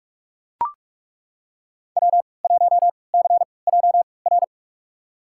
Morse code, heard as a keyed tone, about 42 wpm, spaced as for 25 wpm, 700 Hz. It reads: W1CJR
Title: USA Call Signs - 3x Character Spacing - 45wpm